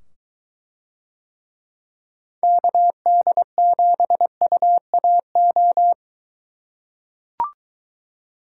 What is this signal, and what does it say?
Morse code, heard as a keyed tone, about 21 wpm, 700 Hz. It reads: KD7UAO